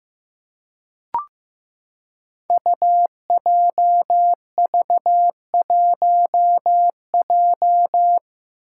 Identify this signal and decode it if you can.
Morse code, heard as a keyed tone, about 15 wpm, 700 Hz. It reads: UJV1J
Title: Five Challenging Letter-Number Combinations 15wpm